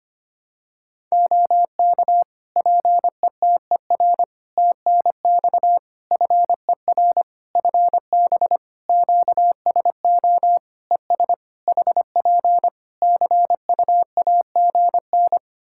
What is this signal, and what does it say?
Morse code, heard as a keyed tone, about 25 wpm, 700 Hz. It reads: OK PETER TNX FER FB QSO ES HP CUAGN
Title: K1USN SST QSO Practice 25wpm